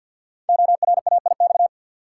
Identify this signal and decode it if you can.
Morse code, heard as a keyed tone, about 53 wpm, 700 Hz. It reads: ORAIX
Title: Code Groups - 5 Letters - 50wpm